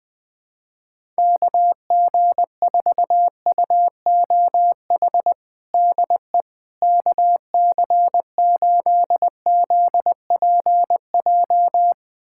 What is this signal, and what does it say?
Morse code, heard as a keyed tone, about 20 wpm, 700 Hz. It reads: KG4UOH DE KC8ZPJ
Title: USA Callsign DE Callsign - 6 Characters - 20wpm